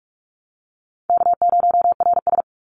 Morse code, heard as a keyed tone, about 46 wpm, 700 Hz. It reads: K0RS
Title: NA SOTA Chaser Callsigns 45wpm